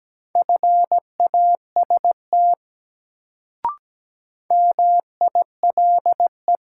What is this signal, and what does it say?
Morse code, heard as a keyed tone, about 17 wpm, 700 Hz. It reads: FAST MILE
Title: Sentences from Top 400 Words - New Word Drill - 17wpm